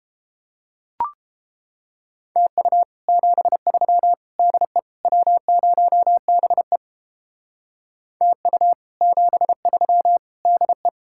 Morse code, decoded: TU 73 DE W0BE TU 73 DE

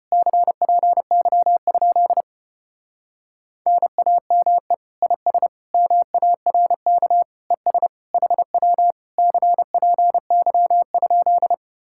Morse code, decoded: CPY? NAME IS MARK ES HW CPY?

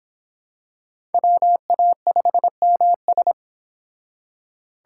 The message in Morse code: WA5MS

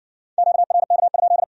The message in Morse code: QRK?

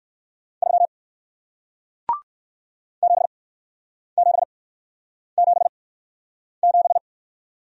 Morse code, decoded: 3 7777

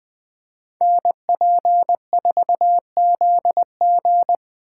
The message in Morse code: NP4ZG

